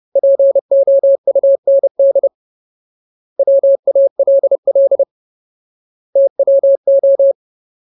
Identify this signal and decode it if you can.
Morse code, heard as a keyed tone, about 30 wpm, 550 Hz. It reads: POUND WALL TWO
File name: Ultimate Word List / 30WPM Ultimate